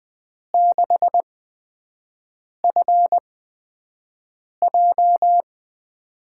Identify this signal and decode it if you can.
Morse code, heard as a keyed tone, about 20 wpm, spaced as for 5 wpm, 700 Hz. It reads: 6FJ
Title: USA Call Signs - 6 Characters - 8x Character Spacing - 20wpm